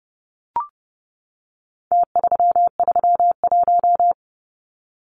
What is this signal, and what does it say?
Morse code, heard as a keyed tone, about 30 wpm, 700 Hz. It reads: T331
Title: Numbers - Contest Style - 4-Digit 30wpm